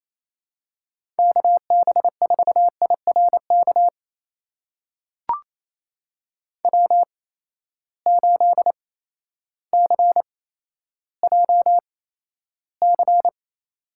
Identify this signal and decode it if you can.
Morse code, heard as a keyed tone, about 28 wpm, spaced as for 13 wpm, 700 Hz. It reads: KB4IRK W8CJC
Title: USA Call Signs - 8x Character Spacing - 28wpm